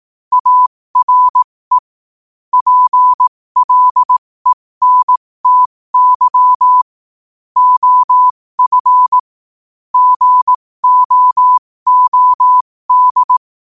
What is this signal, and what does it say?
Morse code, heard as a keyed tone, about 18 wpm, spaced as for 15 wpm, 1000 Hz. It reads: ARE PLENTY OF GOOD
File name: qotdfiles / 15wpm